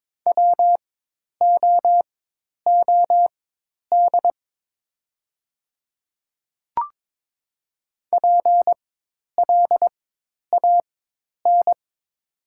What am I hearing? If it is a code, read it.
Morse code, heard as a keyed tone, about 22 wpm, spaced as for 10 wpm, 700 Hz. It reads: WOOD PLAN
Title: 4x Character Spacing - Top 500 Words - 22wpm